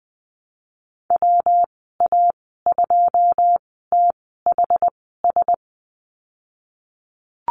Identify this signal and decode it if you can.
Morse code, heard as a keyed tone, about 20 wpm, spaced as for 15 wpm, 700 Hz. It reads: WA2THS
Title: 179 Lesson 29g - Slash - Call Signs 20wpm